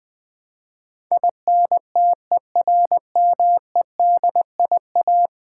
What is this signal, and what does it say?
Morse code, heard as a keyed tone, about 20 wpm, 700 Hz. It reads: INTERMEDIA
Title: Build A Word - 5 Letters At A Time - 12-Letter Words - 20wpm